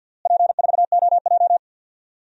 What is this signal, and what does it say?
Morse code, heard as a keyed tone, about 51 wpm, 700 Hz. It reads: P4OJ
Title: DXCC-100 Call Signs - 4 Characters - Repeated 3x - 50wpm